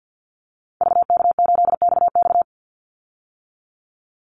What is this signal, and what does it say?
Morse code, heard as a keyed tone, about 50 wpm, 700 Hz. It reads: VK8XX
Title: Call Signs - 50wpm